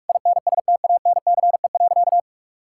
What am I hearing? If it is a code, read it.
Morse code, heard as a keyed tone, about 46 wpm, 700 Hz. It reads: INSTANCE.
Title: Sentences from Top 1000 Words - Review Part 5 - 45wpm